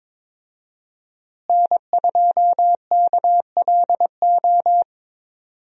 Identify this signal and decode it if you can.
Morse code, heard as a keyed tone, about 22 wpm, 700 Hz. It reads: N2KLO
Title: USA Call Signs - 5 Characters - Repeated 3x - 22wpm